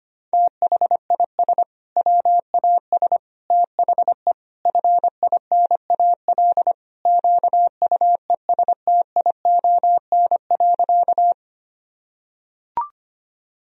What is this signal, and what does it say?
Morse code, heard as a keyed tone, about 25 wpm, 700 Hz. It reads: THIS WAS THE FINAL QUESTION.
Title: Sentences from Top 1000 Words - Review Part 2 - 25wpm